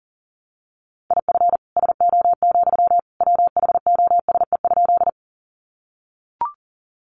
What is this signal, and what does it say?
Morse code, heard as a keyed tone, about 40 wpm, 700 Hz. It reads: IF SO, WHOSE?
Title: Sentences from Top 1200 Words 40wpm